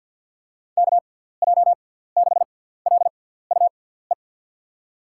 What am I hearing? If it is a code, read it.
Morse code, heard as a keyed tone, about 47 wpm, spaced as for 17 wpm, 700 Hz. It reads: KJ6LUE
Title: USA Call Signs - 6 Characters - 6x Character Spacing - 50wpm